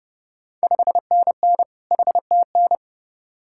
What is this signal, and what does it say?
Morse code, heard as a keyed tone, about 30 wpm, 700 Hz. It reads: 5NN HTN
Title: Ontario QSO Party Exchange 30wpm